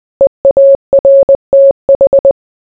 Morse code, heard as a keyed tone, about 20 wpm, 550 Hz. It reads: EARTH